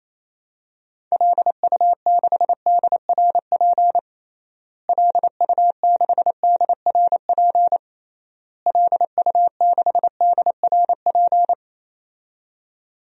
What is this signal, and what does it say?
Morse code, heard as a keyed tone, about 28 wpm, 700 Hz. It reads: LU6DRP LU6DRP LU6DRP
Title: DXCC-100 Call Signs - 6 Characters - Repeated 3x - 28wpm